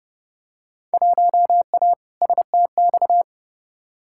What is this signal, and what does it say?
Morse code, heard as a keyed tone, about 32 wpm, 700 Hz. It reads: 1A STX
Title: Field Day Exchanges Encore - 30wpm